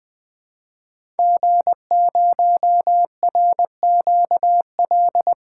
Morse code, heard as a keyed tone, about 20 wpm, 700 Hz. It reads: G0RQL
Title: 2023 SOTA Chaser Call Signs - 20wpm